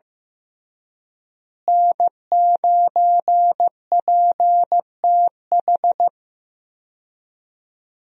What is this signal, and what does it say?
Morse code, heard as a keyed tone, about 15 wpm, 700 Hz. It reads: N9PTH
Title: USA Call Signs - 5 Characters - Repeated 2x - 15wpm